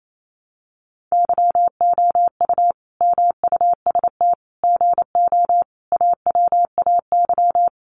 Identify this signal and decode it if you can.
Morse code, heard as a keyed tone, about 28 wpm, 700 Hz. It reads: YOU MUST GO AWAY
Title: Sentences from Top 1100 Words 28wpm